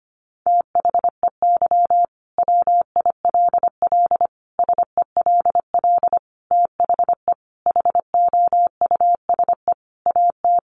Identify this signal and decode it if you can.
Morse code, heard as a keyed tone, about 25 wpm, 700 Hz. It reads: THEY WILL SELL THE HOUSE AT
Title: Sentences from Top 1200 Words 25wpm